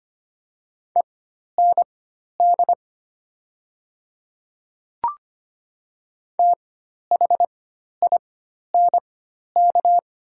Morse code, heard as a keyed tone, about 25 wpm, spaced as for 12 wpm, 700 Hz. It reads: END THINK
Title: 4x Character Spacing - Top 200 Words - 25wpm